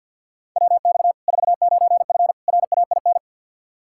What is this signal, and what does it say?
Morse code, heard as a keyed tone, about 53 wpm, 700 Hz. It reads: WX 49F RAIN
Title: Practice QSO - Exchange 2 - Info - 50wpm